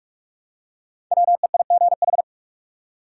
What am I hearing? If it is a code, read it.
Morse code, heard as a keyed tone, about 46 wpm, 700 Hz. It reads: WEIGH
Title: Five Letter Words - 45wpm